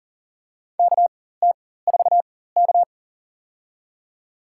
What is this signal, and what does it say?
Morse code, heard as a keyed tone, about 40 wpm, spaced as for 19 wpm, 700 Hz. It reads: KT4K
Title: USA Call Signs - 4 Characters - 4x Character Spacing - 40wpm